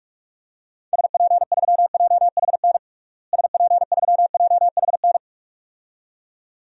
Morse code, heard as a keyed tone, about 43 wpm, 700 Hz. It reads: SP3JHN SP3JHN
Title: DXCC-100 Call Signs - 6 Characters - Repeated 2x - 45wpm